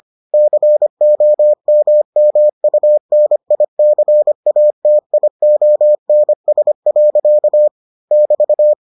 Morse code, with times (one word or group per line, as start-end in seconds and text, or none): 0.3-7.7 COMMUNICATIONS.
8.1-8.7 <BT>